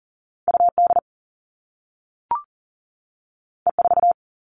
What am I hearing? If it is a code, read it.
Morse code, heard as a keyed tone, about 40 wpm, 700 Hz. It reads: UD E4
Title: All Cut Numbers - 2-Digit 40wpm